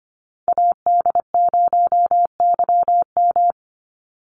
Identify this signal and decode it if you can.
Morse code, heard as a keyed tone, about 25 wpm, 700 Hz. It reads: AD0YM
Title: NA SOTA Chaser Callsigns 25wpm